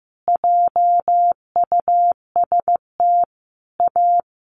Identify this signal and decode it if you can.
Morse code, heard as a keyed tone, about 15 wpm, 700 Hz. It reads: JUST A
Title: Sentences from Top 1200 Words 15wpm